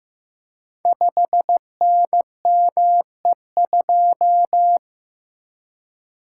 Morse code, heard as a keyed tone, about 15 wpm, 700 Hz. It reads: 5NME2
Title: Code Groups - 5 Letter:Numbers - 15wpm